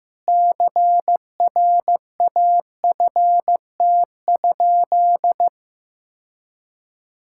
Morse code, heard as a keyed tone, about 15 wpm, 700 Hz. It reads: CRAFT?
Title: Sentences from Top 1000 Words - Review Part 6 - 15wpm